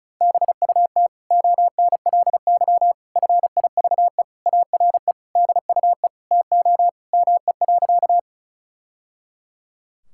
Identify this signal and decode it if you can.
Morse code, heard as a keyed tone, about 34 wpm, 700 Hz. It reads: BUT ONLY FIVE ARE DUE TO ME.
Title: Sentences from Top 1000 Words - Review Part 6 - 35wpm